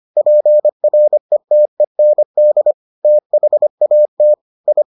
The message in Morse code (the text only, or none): PRETEND THAT I